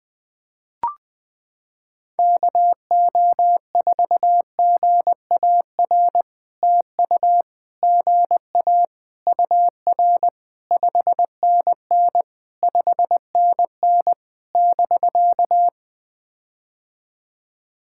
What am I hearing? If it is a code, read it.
Morse code, heard as a keyed tone, about 21 wpm, 700 Hz. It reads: KO4GAR TU GA UR 5NN 5NN <BK>
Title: POTA QSO Practice - 20wpm